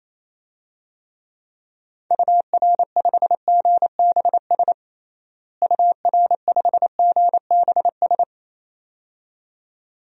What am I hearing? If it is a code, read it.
Morse code, heard as a keyed tone, about 28 wpm, 700 Hz. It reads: UR5GBS UR5GBS
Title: DXCC-100 Call Signs - 6 Characters - Repeated 2x - 28wpm